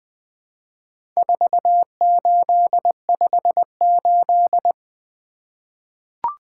Morse code, heard as a keyed tone, about 20 wpm, 700 Hz. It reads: 4858
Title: Four Letter-Number Combinations - 20wpm